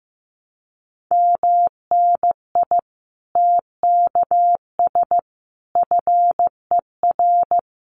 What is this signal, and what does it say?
Morse code, heard as a keyed tone, about 15 wpm, 700 Hz. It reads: MNI TKS FER
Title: Common QSO Phrases 15wpm